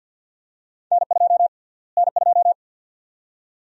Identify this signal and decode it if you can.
Morse code, heard as a keyed tone, about 46 wpm, 700 Hz. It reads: N2 N2
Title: USA Call Sign Prefixes - Repeated 2x - 50wpm